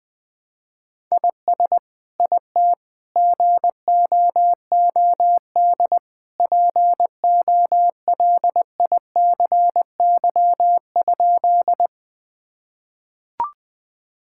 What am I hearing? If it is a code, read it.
Morse code, heard as a keyed tone, about 20 wpm, 700 Hz. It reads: IS IT GOOD POLICY?
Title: Sentences from Top 1000 Words - Review Part 5 - 20wpm